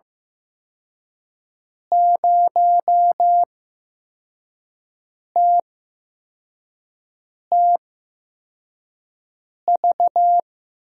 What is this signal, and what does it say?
Morse code, heard as a keyed tone, about 15 wpm, spaced as for 4 wpm, 700 Hz. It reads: 0TTV
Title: USA Call Signs - 6 Characters - 8x Character Spacing - 15wpm